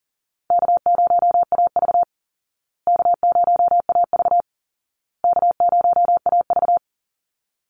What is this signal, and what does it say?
Morse code, heard as a keyed tone, about 40 wpm, 700 Hz. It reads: K0AV K0AV K0AV
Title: Call Signs Repeated 2x - 40wpm